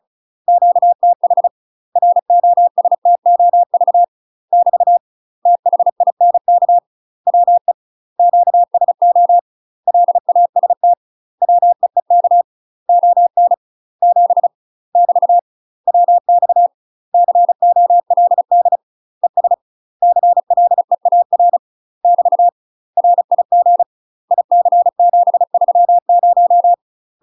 Morse code, heard as a keyed tone, about 34 wpm, 700 Hz. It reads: QTH ROSTOV <BT> THINK WE QSO LAST WEEK ON 7 <BT> WX COLD ES CLEAR <BT> RIG IC730
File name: UK MASTER QSOs TRAINING / 35WPM / UK MASTER QSOs 0030